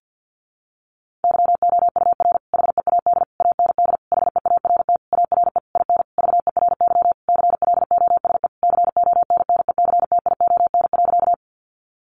Morse code, heard as a keyed tone, about 50 wpm, 700 Hz. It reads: YOUR HEAD AND HEART ARE IN VERY CLOSE CONNECTION.
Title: Sentences from Top 1300 Words 50wpm